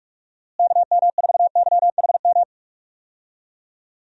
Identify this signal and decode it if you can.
Morse code, heard as a keyed tone, about 41 wpm, 700 Hz. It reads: KM4YHM
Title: USA Call Signs - 6 Characters - Repeated 3x - 45wpm